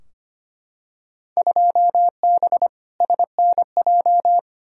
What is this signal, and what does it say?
Morse code, heard as a keyed tone, about 25 wpm, 700 Hz. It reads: 2B SNJ
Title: Field Day Exchanges Encore - 25wpm